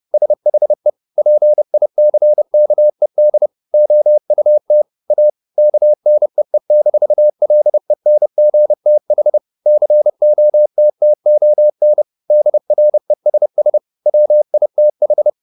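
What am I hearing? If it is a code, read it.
Morse code, heard as a keyed tone, about 30 wpm, 600 Hz. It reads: SHE PICKED OUT A KNEE-LENGTH COTTON DRESS WITH
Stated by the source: Morse code at 30 wpm, the novel Never by Ken Follett in English, from CW med Gnister.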